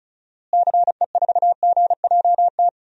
Morse code, heard as a keyed tone, about 36 wpm, 700 Hz. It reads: CE4GJT
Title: DXCC-100 Call Signs - 6 Characters - Repeated 3x - 35wpm